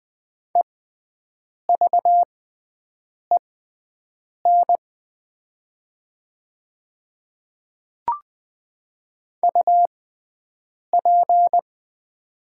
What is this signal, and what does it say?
Morse code, heard as a keyed tone, about 20 wpm, spaced as for 7 wpm, 700 Hz. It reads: EVEN UP
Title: 6x Character Spacing - Top 300 Words - 20wpm